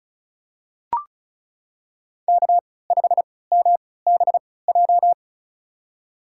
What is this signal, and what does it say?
Morse code, heard as a keyed tone, about 36 wpm, spaced as for 20 wpm, 700 Hz. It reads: K5MBJ
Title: USA Call Signs - 5 Characters - 3x Character Spacing - 35wpm